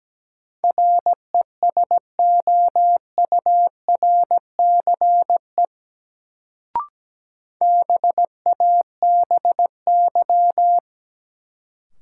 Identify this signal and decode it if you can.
Morse code, heard as a keyed tone, about 17 wpm, 700 Hz. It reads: RESOURCE BABY
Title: Sentences from Top 1300 Words - New Word Drill Alt - 17wpm